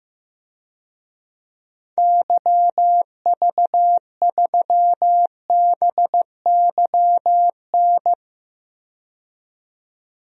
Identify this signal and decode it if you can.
Morse code, heard as a keyed tone, about 15 wpm, 700 Hz. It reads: YV3BYN
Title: DXCC-100 Call Signs - 6 Characters - 15wpm